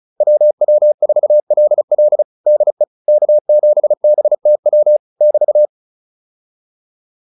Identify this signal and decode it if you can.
Morse code, heard as a keyed tone, about 34 wpm, 600 Hz. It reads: WW4LL DE K7BTW <BT>
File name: QSO203 35